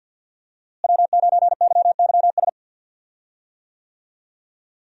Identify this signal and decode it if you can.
Morse code, heard as a keyed tone, about 52 wpm, 700 Hz. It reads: W9YYS
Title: USA Call Signs - 5 Characters - Repeated 2x - 50wpm